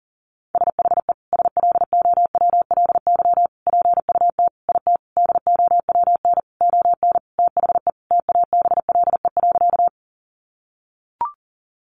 Morse code, heard as a keyed tone, about 40 wpm, 700 Hz. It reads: SHE SLOWLY PUT IT DOWN ON THE TABLE.
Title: Sentences from Top 1100 Words 40wpm